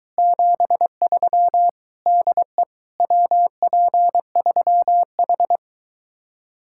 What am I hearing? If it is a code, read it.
Morse code, heard as a keyed tone, about 22 wpm, 700 Hz. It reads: 73 DE WP3H